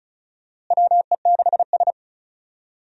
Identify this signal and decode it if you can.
Morse code, heard as a keyed tone, about 36 wpm, 700 Hz. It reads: WE6S
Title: USA Call Signs - 4 Characters - Repeated 3x - 35wpm